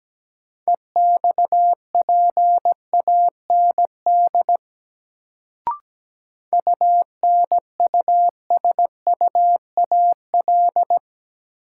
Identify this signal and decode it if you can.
Morse code, heard as a keyed tone, about 17 wpm, 700 Hz. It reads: EXPAND UNUSUAL
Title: Sentences from Top 1400 Words - New Word Drill Alt - 17wpm